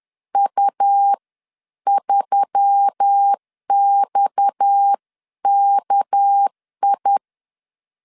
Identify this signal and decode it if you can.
Morse code, heard as a keyed tone, about 10 wpm, spaced as for 9 wpm, 800 Hz. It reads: U3XKI